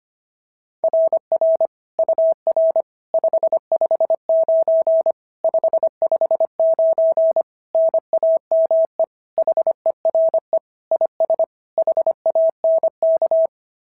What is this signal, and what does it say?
Morse code, heard as a keyed tone, about 25 wpm, 660 Hz. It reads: RR UR 559 559 NAME HERE IS HANK
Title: SOTA QSO Practice 25wpm